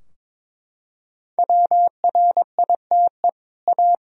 Morse code, heard as a keyed tone, about 22 wpm, 700 Hz. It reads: WRITE A
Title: Sentences from Top 1000 Words - Review Part 3 - 22wpm